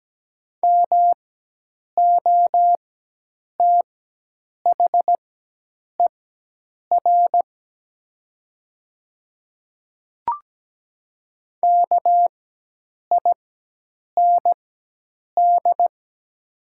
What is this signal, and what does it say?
Morse code, heard as a keyed tone, about 17 wpm, spaced as for 8 wpm, 700 Hz. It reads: MOTHER KIND
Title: 4x Character Spacing - Top 200 Words - 17wpm